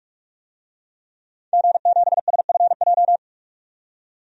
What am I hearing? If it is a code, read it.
Morse code, heard as a keyed tone, about 46 wpm, 700 Hz. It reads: G7SFJ
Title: DXCC-100 Call Signs - 5 Characters - 45wpm